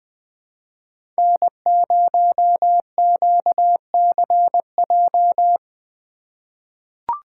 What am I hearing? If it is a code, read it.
Morse code, heard as a keyed tone, about 20 wpm, 700 Hz. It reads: N0QCJ
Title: USA Call Signs - 5 Characters - 4x Character Spacing - 20wpm